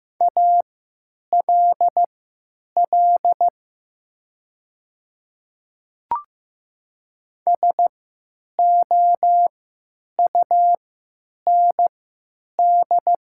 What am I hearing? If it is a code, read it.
Morse code, heard as a keyed tone, about 15 wpm, spaced as for 9 wpm, 700 Hz. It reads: ALL SOUND
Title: 3x Character Spacing - Top 100 Words - 15wpm